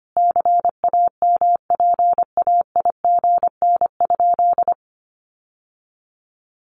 Morse code, heard as a keyed tone, about 25 wpm, 700 Hz. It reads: CAMPAIGN?
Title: Sentences from Top 1200 Words 25wpm